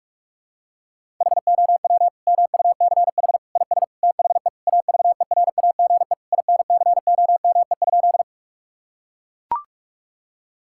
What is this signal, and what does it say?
Morse code, heard as a keyed tone, about 41 wpm, 700 Hz. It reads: HOW MUCH IS THE AVERAGE INCOME?
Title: Sentences from Top 1000 Words - Review Part 6 - 45wpm